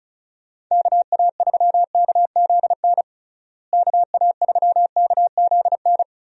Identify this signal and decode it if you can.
Morse code, heard as a keyed tone, about 34 wpm, 700 Hz. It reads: KA3KZN KA3KZN
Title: USA Call Signs - 6 Characters - Repeated 3x - 35wpm